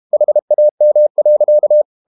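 Morse code, heard as a keyed tone, about 32 wpm, 600 Hz. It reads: HAM.
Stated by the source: High-speed QRQ CW at 32 WPM.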